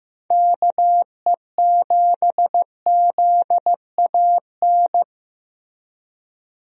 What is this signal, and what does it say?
Morse code, heard as a keyed tone, about 16 wpm, 700 Hz. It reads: KE7ZAN
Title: USA Call Signs - 6 Characters - 15wpm